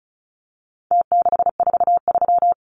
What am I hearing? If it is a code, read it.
Morse code, heard as a keyed tone, about 34 wpm, 700 Hz. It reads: T643
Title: Numbers - Contest Style - 4-Digit 35wpm